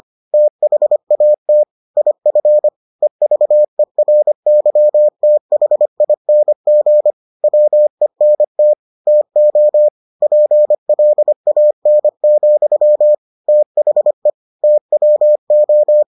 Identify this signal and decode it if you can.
Morse code, heard as a keyed tone, about 25 wpm, 600 Hz. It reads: THAT IF EVERYTHING WENT TO PLAN, THE TWO